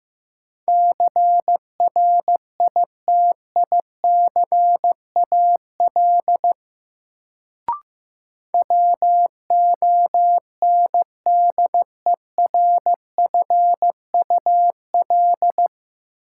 Morse code, heard as a keyed tone, about 15 wpm, 700 Hz. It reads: CRITICAL WONDERFUL
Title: Sentences from Top 1100 Words - New Word Drill - 15wpm